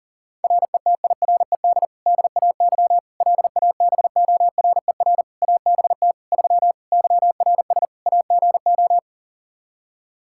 Morse code, heard as a keyed tone, about 40 wpm, 700 Hz. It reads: RETIRED DAY LABORER ABT 3 YRS AGO
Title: Practice QSO - Exchange 3 - More Info - 40wpm